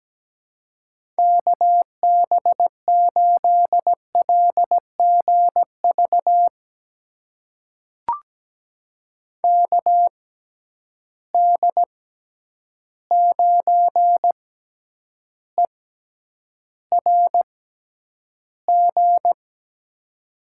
Morse code, heard as a keyed tone, about 17 wpm, spaced as for 9 wpm, 700 Hz. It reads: KB8LGV KD9ERG
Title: USA Call Signs - 6 Characters - 6x Character Spacing - 17wpm